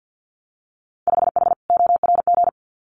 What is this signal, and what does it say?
Morse code, heard as a keyed tone, about 50 wpm, 700 Hz. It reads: 5H ORG
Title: Winter Field Day Exchanges Encore 50wpm